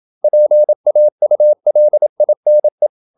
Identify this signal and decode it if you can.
Morse code, heard as a keyed tone, about 27 wpm, 600 Hz. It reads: PAULINE
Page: Never 26wpm | CW med Gnister